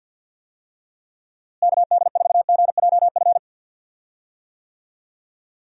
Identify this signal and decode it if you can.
Morse code, heard as a keyed tone, about 51 wpm, 700 Hz. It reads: KD4GJF
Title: USA Call Signs - 6 Characters - Repeated 3x - 50wpm